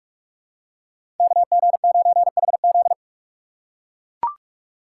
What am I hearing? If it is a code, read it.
Morse code, heard as a keyed tone, about 44 wpm, 690 Hz. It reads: KG9HZ
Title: USA Call Signs - 6x Character Spacing - 45wpm